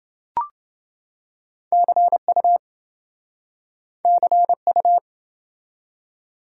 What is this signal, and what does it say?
Morse code, heard as a keyed tone, about 26 wpm, 700 Hz. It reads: CU CU